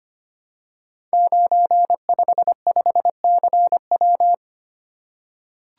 Morse code, heard as a keyed tone, about 25 wpm, 700 Hz. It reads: 955CW
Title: Code Groups - 5 Letter:Numbers - 25wpm